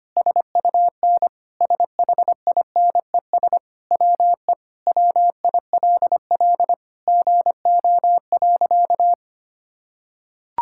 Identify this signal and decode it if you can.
Morse code, heard as a keyed tone, about 25 wpm, 700 Hz. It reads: SUN SHINES WE WILL GO.
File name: practice / 25wpm / Sentences with 4-Word Phrases - 25wpm